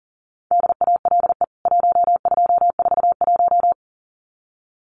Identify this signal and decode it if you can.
Morse code, heard as a keyed tone, about 40 wpm, 700 Hz. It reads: DALE 1241
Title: CWT Exchange 40wpm